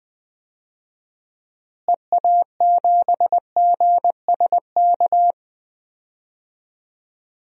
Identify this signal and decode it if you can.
Morse code, heard as a keyed tone, about 20 wpm, 700 Hz. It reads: EA7GSK
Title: DXCC-100 Call Signs - 6 Characters - Repeated 3x - 20wpm